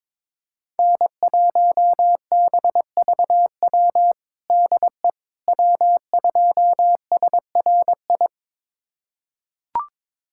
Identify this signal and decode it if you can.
Morse code, heard as a keyed tone, about 22 wpm, 700 Hz. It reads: N1BVW DE W2SRI
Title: USA Callsign DE Callsign - 5 Characters - 22wpm